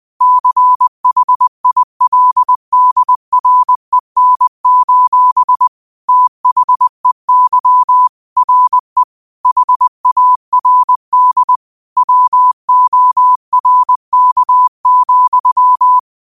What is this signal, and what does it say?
Morse code, heard as a keyed tone, about 20 wpm, 1000 Hz. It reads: CHILDREN: THEY RE HARD WORK,
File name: qotdfiles / 20wpm